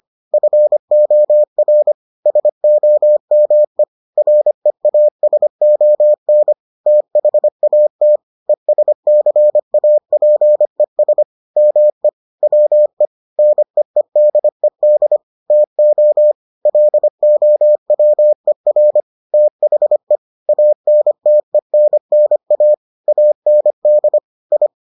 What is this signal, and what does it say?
Morse code, heard as a keyed tone, about 25 wpm, 600 Hz. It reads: FOR SOME REASON THAT ESCAPES ME WE NEEDED TO LOWER THE ANTENNA AND I